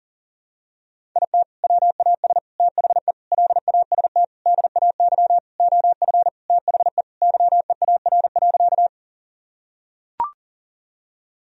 Morse code, heard as a keyed tone, about 40 wpm, 700 Hz. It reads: IT WAS THE LAST DAY OF THE YEAR.
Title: Sentences from Top 1000 Words - Review Part 1 - 40wpm